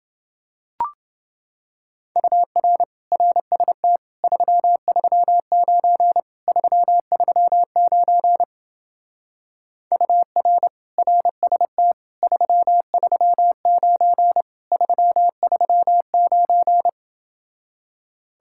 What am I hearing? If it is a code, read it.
Morse code, heard as a keyed tone, about 28 wpm, 700 Hz. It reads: UR RST 339 339 UR RST 339 339